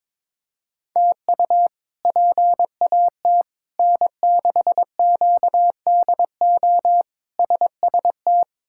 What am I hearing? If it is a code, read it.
Morse code, heard as a keyed tone, about 22 wpm, 700 Hz. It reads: TU PAT N6QDO SST
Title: K1USN SST QSO Practice - 22wpm